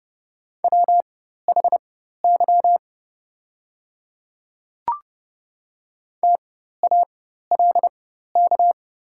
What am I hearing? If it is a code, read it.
Morse code, heard as a keyed tone, about 30 wpm, spaced as for 14 wpm, 700 Hz. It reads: WHY TALK
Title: 4x Character Spacing - Top 500 Words - 30wpm